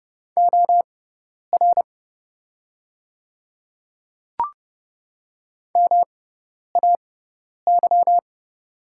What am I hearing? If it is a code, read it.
Morse code, heard as a keyed tone, about 30 wpm, spaced as for 10 wpm, 700 Hz. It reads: OR MAY
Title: 6x Character Spacing - Top 100 Words - 30wpm